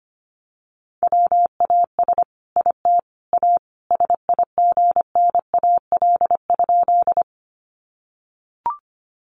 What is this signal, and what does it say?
Morse code, heard as a keyed tone, about 25 wpm, 700 Hz. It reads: WAS IT A SIGNAL?
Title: Sentences from Top 1400 Words 25wpm